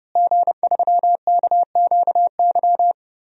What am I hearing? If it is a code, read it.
Morse code, heard as a keyed tone, about 30 wpm, 700 Hz. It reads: G3KQY
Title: DXCC-100 Call Signs - 5 Characters - Repeated 3x - 30wpm